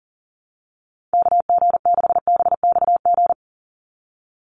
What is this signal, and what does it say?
Morse code, heard as a keyed tone, about 39 wpm, 700 Hz. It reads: KG6BXG
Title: USA Call Signs 40wpm